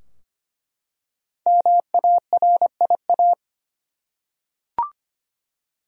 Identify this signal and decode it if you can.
Morse code, heard as a keyed tone, about 25 wpm, 700 Hz. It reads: MARIA
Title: USA Common Names - Max 5 Letters - 25wpm